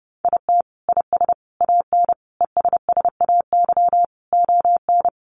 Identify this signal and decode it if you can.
Morse code, heard as a keyed tone, about 30 wpm, 700 Hz. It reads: IT IS AN ESSAY ON
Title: Sentences from Top 1500 Words 30wpm